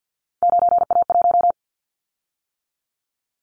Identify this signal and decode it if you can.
Morse code, heard as a keyed tone, about 50 wpm, 700 Hz. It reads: 9A1
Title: Callsign Prefixes for Top 50 DXCC - 50wpm